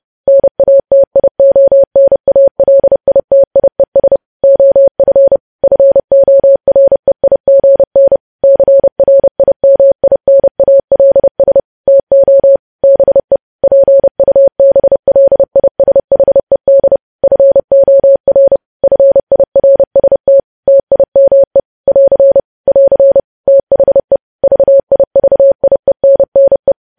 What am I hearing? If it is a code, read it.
Morse code, heard as a keyed tone, about 30 wpm, 550 Hz. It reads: NATIONALITIES OF FOREIGN CRIMINALS TO BE PUBLISHED FOR FIRST TIME <AR> <AR> THE VIVIENNE
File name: podcast / News headlines-1x-30x30